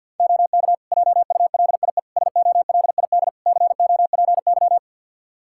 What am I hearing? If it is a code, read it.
Morse code, heard as a keyed tone, about 50 wpm, 700 Hz. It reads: OK JULIE SOLID COPY
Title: Practice QSO - Exchange 3 - More Info - 50wpm